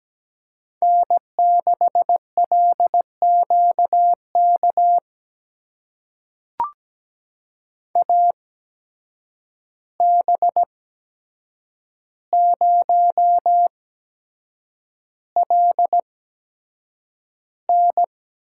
Morse code, heard as a keyed tone, about 17 wpm, spaced as for 7 wpm, 700 Hz. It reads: N6LQK AB0LN
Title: USA Call Signs - 5 Characters - 8x Character Spacing - 17wpm